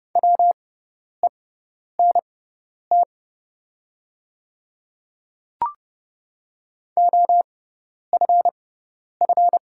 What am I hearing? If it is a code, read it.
Morse code, heard as a keyed tone, about 30 wpm, spaced as for 10 wpm, 700 Hz. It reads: WENT OFF
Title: 6x Character Spacing - Top 200 Words - 30wpm